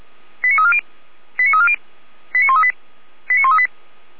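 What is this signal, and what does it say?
Five-tone selective call (ZVEI), heard as quick runs of five tones, tones of about 70 ms. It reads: 89340 89340 89138 89138
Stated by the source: ZVEI five-tone selective call received on SDR.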